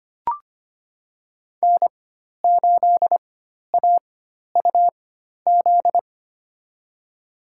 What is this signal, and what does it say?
Morse code, heard as a keyed tone, about 25 wpm, spaced as for 12 wpm, 700 Hz. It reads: N8AUZ
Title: USA Call Signs - 5 Characters - 4x Character Spacing - 25wpm